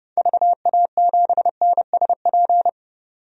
Morse code, heard as a keyed tone, about 30 wpm, 700 Hz. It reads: VA7NSP
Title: DXCC-100 Call Signs - 6 Characters - 30wpm